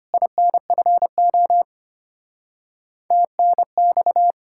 Morse code, heard as a keyed tone, about 32 wpm, 700 Hz. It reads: INFO TNX